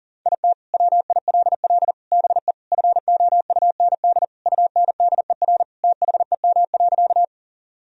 Morse code, heard as a keyed tone, about 40 wpm, 700 Hz. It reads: IT WILL BE FOUND UNDER THEM.
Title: Sentences from Top 1000 Words - Review Part 1 - 40wpm